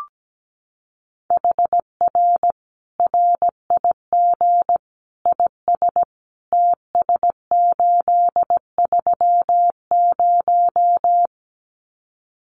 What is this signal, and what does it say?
Morse code, heard as a keyed tone, about 17 wpm, 700 Hz. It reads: HR RIG IS TS830
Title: Common QSO Phrases 17wpm